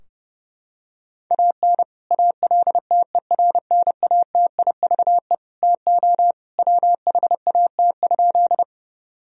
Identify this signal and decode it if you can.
Morse code, heard as a keyed tone, about 30 wpm, 700 Hz. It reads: AN ALTERNATIVE TO WHAT?
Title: Sentences from Top 1000 Words - Review Part 4 - 30wpm